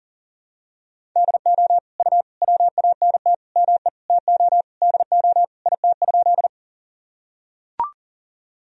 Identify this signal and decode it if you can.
Morse code, heard as a keyed tone, about 40 wpm, 700 Hz. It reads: DO U WANT ME TO DO IT?
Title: Sentences from Top 200 Words and Abbreviations 40wpm